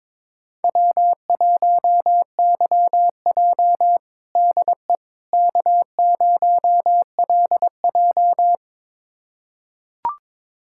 Morse code, heard as a keyed tone, about 22 wpm, 700 Hz. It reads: W1YJ DE K0LJ
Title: USA Callsign DE Callsign - 4 Characters - 22wpm